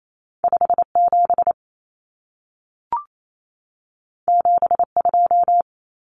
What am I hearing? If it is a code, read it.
Morse code, heard as a keyed tone, about 28 wpm, 700 Hz. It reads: 57 72
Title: Common Cut Numbers - 2-Digit 28wpm